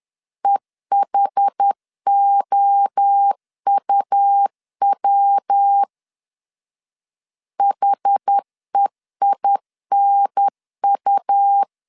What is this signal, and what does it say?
Morse code, heard as a keyed tone, about 10 wpm, 800 Hz. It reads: EHOUW HEINU
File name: CW / DET / 1 3 1500C S C 10wpm 3335